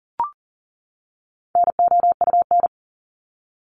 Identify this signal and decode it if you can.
Morse code, heard as a keyed tone, about 40 wpm, 700 Hz. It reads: NOUN
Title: Top 500 Words 40wpm